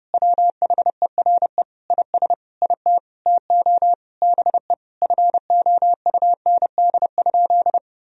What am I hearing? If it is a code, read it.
Morse code, heard as a keyed tone, about 30 wpm, 700 Hz. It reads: WHERE IS IT TO BE FOUND?
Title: Sentences from Top 300 Words and Abbreviations 30wpm